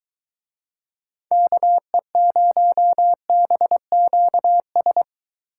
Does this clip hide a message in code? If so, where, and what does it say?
Yes: Morse code from 1.3 to 5.0 s, heard as a keyed tone, about 21 wpm, 700 Hz. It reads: KE0BQS